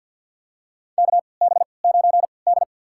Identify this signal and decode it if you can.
Morse code, heard as a keyed tone, about 52 wpm, spaced as for 29 wpm, 700 Hz. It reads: KB9D
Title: USA Call Signs - 4 Characters - 3x Character Spacing - 50wpm